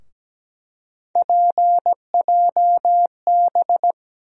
Morse code, heard as a keyed tone, about 17 wpm, 700 Hz. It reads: PJB
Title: USA Call Sign Postfixes - 3 Letters 17wpm